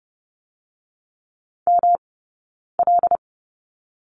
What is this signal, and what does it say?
Morse code, heard as a keyed tone, about 30 wpm, 700 Hz. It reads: M L
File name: Lesson 18d - Letter m - Two Characters 30wpm